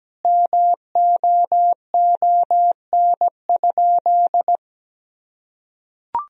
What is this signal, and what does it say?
Morse code, heard as a keyed tone, about 17 wpm, 700 Hz. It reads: MOON?
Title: Sentences from Top 1000 Words - Review Part 3 - 17wpm